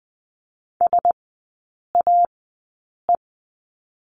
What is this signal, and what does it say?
Morse code, heard as a keyed tone, about 20 wpm, spaced as for 8 wpm, 700 Hz. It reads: SAE
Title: 041 Lesson 8e - Number 1 - Three Characters 20wpm